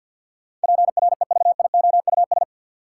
Morse code, heard as a keyed tone, about 53 wpm, 700 Hz. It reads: PREVIOUS
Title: Sentences from Top 1500 Words - New Word Drill Alt - 50wpm